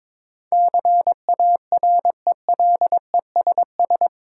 Morse code, heard as a keyed tone, about 21 wpm, 700 Hz. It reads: CARELESS
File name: practice / 22wpm / Eight Letter Words - 22wpm